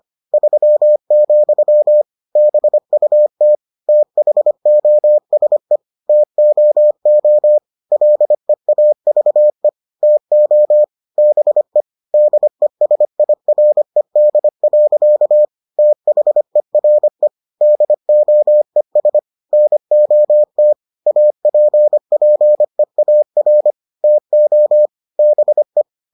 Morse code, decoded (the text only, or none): 3, BUT THOSE TOO LEAVE TO BE DESIRED. THERE DOES NOT APPEAR TO BE